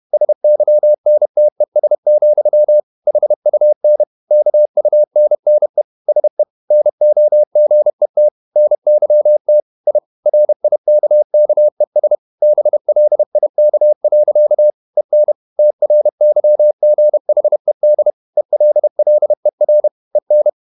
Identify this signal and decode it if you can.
Morse code, heard as a keyed tone, about 31 wpm, 600 Hz. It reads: SYNTES, HUN KUNNE SE NOGET NYT I RIKKES BLIK. EN TRYGHED ELLER EN
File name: Ferie QRQ 31wpm